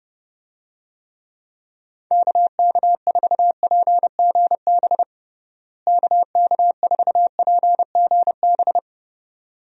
Morse code, heard as a keyed tone, about 30 wpm, 700 Hz. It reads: KK4PGB KK4PGB
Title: USA Call Signs - 6 Characters - Repeated 2x - 30wpm